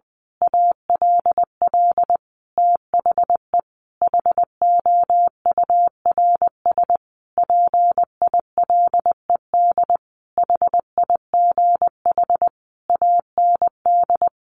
Morse code, heard as a keyed tone, about 20 wpm, 700 Hz. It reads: ALL THE HOURS PILED HIGH AND
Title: A Sound Of Thunder 20wpm